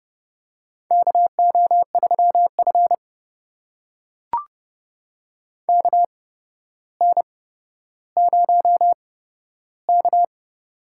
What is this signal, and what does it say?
Morse code, heard as a keyed tone, about 30 wpm, spaced as for 13 wpm, 700 Hz. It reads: KO3F KN0K
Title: USA Call Signs - 4 Characters - 8x Character Spacing - 30wpm